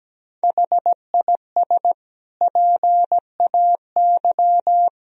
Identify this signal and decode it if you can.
Morse code, heard as a keyed tone, about 17 wpm, 700 Hz. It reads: HIS PAY
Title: Sentences from Top 1000 Words - Review Part 1 - 17wpm